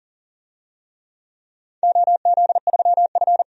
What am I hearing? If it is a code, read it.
Morse code, heard as a keyed tone, about 40 wpm, 700 Hz. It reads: OZ3F
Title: DXCC-100 Call Signs - 4 Characters - 40wpm